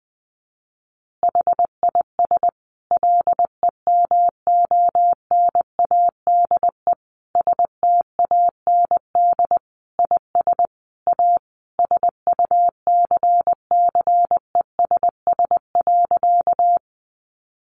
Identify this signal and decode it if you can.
Morse code, heard as a keyed tone, about 20 wpm, 700 Hz. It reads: HIS LEMONADE STAND IS A SUCCESS.
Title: Additional Easy Phrases 20wpm